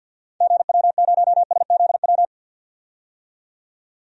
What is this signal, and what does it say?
Morse code, heard as a keyed tone, about 53 wpm, 700 Hz. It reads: GW0SZW
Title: DXCC-100 Call Signs - 6 Characters - 50wpm